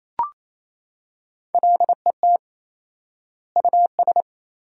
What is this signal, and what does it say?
Morse code, heard as a keyed tone, about 27 wpm, 700 Hz. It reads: LET US
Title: 4x Word-Spacing - Sentences from Top 400 Words - 28wpm